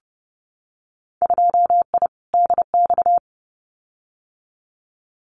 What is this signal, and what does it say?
Morse code, heard as a keyed tone, about 29 wpm, 700 Hz. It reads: 2I DX
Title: Winter Field Day Exchanges 30wpm